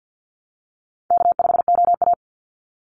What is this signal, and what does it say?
Morse code, heard as a keyed tone, about 50 wpm, 700 Hz. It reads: K5OA
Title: Easy Call Signs - 50wpm